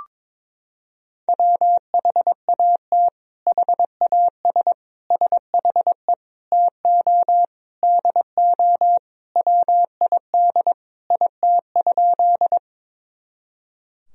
Morse code, heard as a keyed tone, about 22 wpm, 700 Hz. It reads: WHAT HAS SHE TO DO WID IT?
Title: Sentences from Top 100 Words and Abbreviations 22wpm